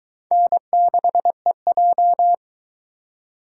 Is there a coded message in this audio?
Yes: Morse code from 0.3 to 2.3 s, heard as a keyed tone, about 23 wpm, 700 Hz. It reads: N6EJ